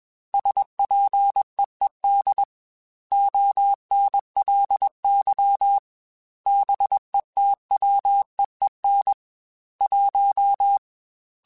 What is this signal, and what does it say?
Morse code, heard as a keyed tone, about 21 wpm, 800 Hz. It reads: SPEED ONLY BETWEEN 1
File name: Weather Sets / Weather Sets-mp3s / Weather 5 1 @20